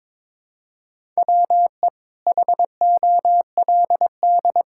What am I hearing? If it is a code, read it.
Morse code, heard as a keyed tone, about 22 wpm, 700 Hz. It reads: WE HOLD
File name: Sets of 9 Words - 22wpm